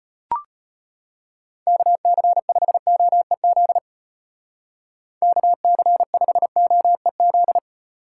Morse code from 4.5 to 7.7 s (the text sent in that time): KC5OEZ